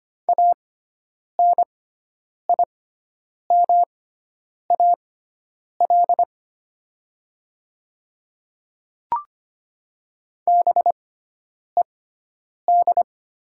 Morse code, heard as a keyed tone, about 25 wpm, spaced as for 9 wpm, 690 Hz. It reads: ANIMAL BED
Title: 6x Character Spacing - Top 500 Words - 25wpm